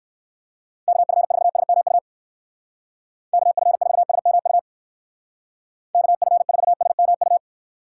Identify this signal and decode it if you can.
Morse code, heard as a keyed tone, about 59 wpm, 700 Hz. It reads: KF4SMU KF4SMU KF4SMU